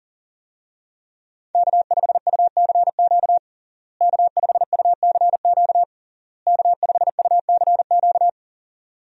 Morse code, heard as a keyed tone, about 40 wpm, 700 Hz. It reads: K5UCQ K5UCQ K5UCQ
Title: USA Call Signs - 5 Characters - Repeated 3x - 40wpm